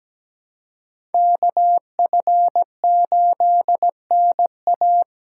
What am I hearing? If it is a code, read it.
Morse code, heard as a keyed tone, about 20 wpm, 700 Hz. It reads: KF8NA
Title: Speed Racing - USA 2x2 Call Signs 17wpm